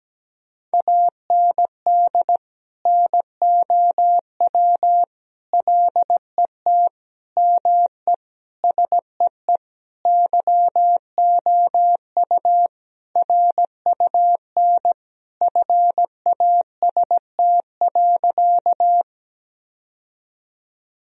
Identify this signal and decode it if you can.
Morse code, heard as a keyed tone, about 17 wpm, 700 Hz. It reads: AND NOW LET ME SEE YOU RUN FAST.
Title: Sentences from Top 1000 Words - Review Part 2 - 17wpm